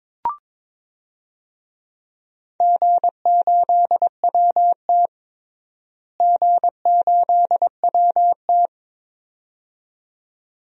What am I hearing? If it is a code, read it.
Morse code, heard as a keyed tone, about 21 wpm, 700 Hz. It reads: G8WT G8WT
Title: DXCC-100 Call Signs - 4 Characters - Repeated 2x - 22wpm